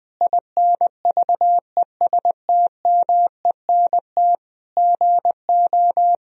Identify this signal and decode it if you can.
Morse code, heard as a keyed tone, about 20 wpm, 700 Hz. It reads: INVESTMENT GO
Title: Sentences from Top 1000 Words - Review Part 4 - 20wpm